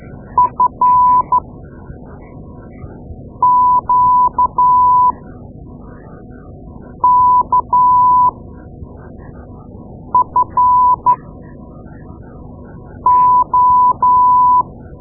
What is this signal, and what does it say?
Morse code, heard as a keyed tone, about 13 wpm, spaced as for 4 wpm, 1000 Hz. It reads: FQKFO
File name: Morse Lessons / Morse Lessons / Tape-2A